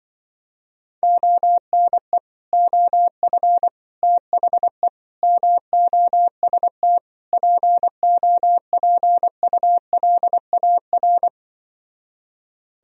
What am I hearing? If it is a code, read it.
Morse code, heard as a keyed tone, about 24 wpm, 700 Hz. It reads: ONE OF THE MOST POPULAR
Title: Speed-Racing - Sets of 5 Words 20wpm